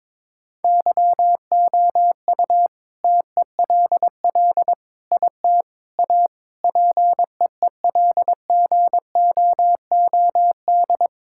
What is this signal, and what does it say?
Morse code, heard as a keyed tone, about 22 wpm, 690 Hz. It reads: YOU TELL IT A PEELGOOD
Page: morse-code-ninja-practice-22wpm